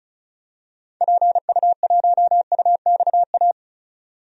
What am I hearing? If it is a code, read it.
Morse code, heard as a keyed tone, about 36 wpm, 700 Hz. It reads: PU1UXA
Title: DXCC-100 Call Signs - 6 Characters - 35wpm